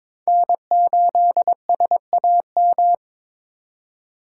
Morse code, heard as a keyed tone, about 22 wpm, 700 Hz. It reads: N8SAM
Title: USA Call Signs - 4x Character Spacing - 22wpm